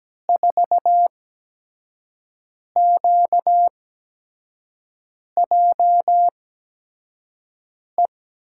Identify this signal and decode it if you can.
Morse code, heard as a keyed tone, about 17 wpm, spaced as for 5 wpm, 700 Hz. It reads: 4QJE
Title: USA Call Signs - 6 Characters - 8x Character Spacing - 17wpm